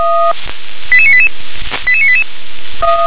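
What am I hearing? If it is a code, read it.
Five-tone selective call (ZVEI), heard as quick runs of five tones, tones of about 70 ms. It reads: 88988 88988
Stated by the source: ZVEI five-tone selective call received by SDR.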